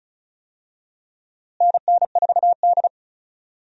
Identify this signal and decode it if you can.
Morse code, heard as a keyed tone, about 36 wpm, 700 Hz. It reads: NN4D
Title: USA Call Signs - 4 Characters - Repeated 3x - 35wpm